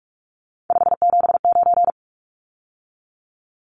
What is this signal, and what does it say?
Morse code, heard as a keyed tone, about 44 wpm, 700 Hz. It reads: 579
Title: Numbers - 3-Digit 45wpm